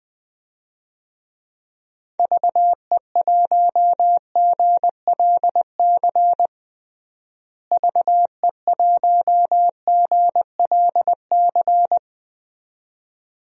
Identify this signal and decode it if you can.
Morse code, heard as a keyed tone, about 20 wpm, 700 Hz. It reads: VE1GLC VE1GLC
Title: DXCC-100 Call Signs - 6 Characters - Repeated 2x - 20wpm